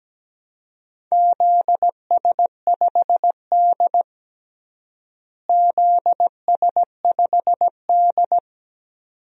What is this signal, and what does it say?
Morse code, heard as a keyed tone, about 17 wpm, 700 Hz. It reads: ZS5D ZS5D
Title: DXCC-100 Call Signs - 4 Characters - Repeated 2x - 17wpm